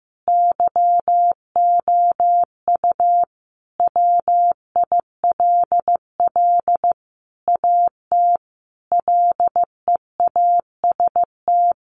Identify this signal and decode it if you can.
Morse code, heard as a keyed tone, about 15 wpm, 700 Hz. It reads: YOU WILL AT LEAST
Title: Sentences from Top 1500 Words 15wpm